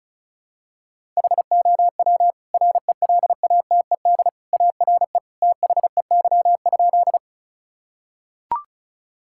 Morse code, heard as a keyed tone, about 36 wpm, 700 Hz. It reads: HOW RELATED ARE THEY?
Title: Sentences from Top 1000 Words - Review Part 5 - 35wpm